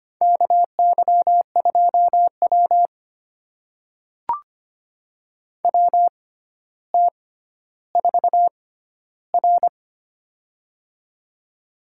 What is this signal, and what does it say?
Morse code, heard as a keyed tone, about 25 wpm, spaced as for 13 wpm, 700 Hz. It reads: KY2W WT4R
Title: USA Call Signs - 4 Characters - 6x Character Spacing - 25wpm